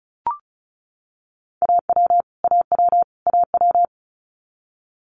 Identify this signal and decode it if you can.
Morse code, heard as a keyed tone, about 36 wpm, 700 Hz. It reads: AW AW AW
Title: Multi-letter Phonemes Repeated 3x 35wpm